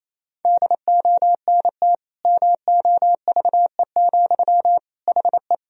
Morse code, heard as a keyed tone, about 28 wpm, 700 Hz. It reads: DONT MOVE, HE
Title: Sentences from Top 300 Words and Abbreviations 28wpm